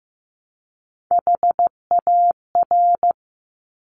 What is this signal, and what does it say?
Morse code, heard as a keyed tone, about 15 wpm, 700 Hz. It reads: HAR
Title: Top Three Letter Combinations in Words 15wpm